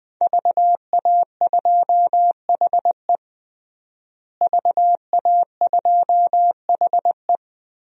Morse code, heard as a keyed tone, about 20 wpm, 700 Hz. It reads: VA2HE VA2HE
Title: DXCC-100 Call Signs - 5 Characters - Repeated 2x - 20wpm